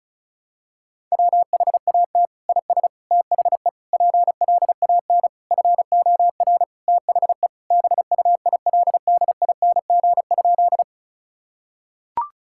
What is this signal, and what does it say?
Morse code, heard as a keyed tone, about 34 wpm, 700 Hz. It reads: WHAT IS THE PLAN FOR THE BUILDING?
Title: Sentences from Top 1000 Words - Review Part 5 - 35wpm